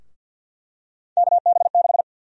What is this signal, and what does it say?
Morse code, heard as a keyed tone, about 52 wpm, 700 Hz. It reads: KB6
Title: USA Call Sign Prefixes - Repeated 2x - 50wpm